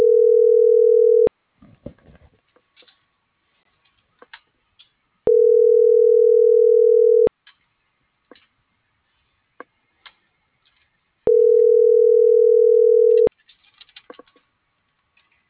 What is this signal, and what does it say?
Ringback tone (North America), 440 + 480 Hz.